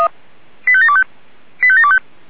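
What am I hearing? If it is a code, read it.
Five-tone selective call (ZVEI), heard as quick runs of five tones, tones of about 70 ms. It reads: 85725 85725
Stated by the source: ZVEI five-tone selective call received on SDR.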